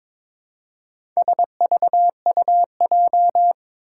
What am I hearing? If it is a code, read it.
Morse code, heard as a keyed tone, about 22 wpm, 700 Hz. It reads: SVUJ
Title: Four Challenging Letter-Number Combinations 22wpm